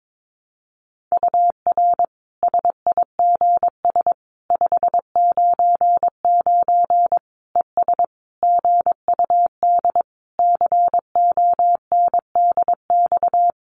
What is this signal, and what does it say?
Morse code, heard as a keyed tone, about 22 wpm, 700 Hz. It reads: UR SIGS 599 ES GUD CONDX
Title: Common QSO Phrases 22wpm